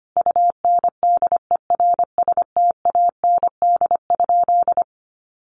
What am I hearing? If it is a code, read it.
Morse code, heard as a keyed tone, about 25 wpm, 700 Hz. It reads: UNDERSTAND?
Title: Sentences from Top 600 Words 25wpm